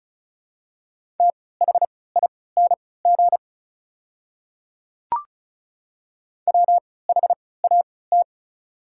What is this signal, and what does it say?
Morse code, heard as a keyed tone, about 36 wpm, spaced as for 20 wpm, 700 Hz. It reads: THING WHAT
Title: 3x Character Spacing - Top 200 Words - 35wpm